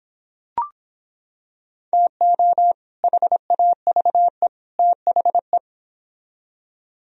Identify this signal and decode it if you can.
Morse code, heard as a keyed tone, about 26 wpm, 700 Hz. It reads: TO HAVE THE